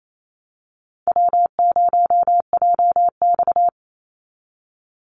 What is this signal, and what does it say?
Morse code, heard as a keyed tone, about 28 wpm, 700 Hz. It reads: W0JX
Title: Call Signs - 28wpm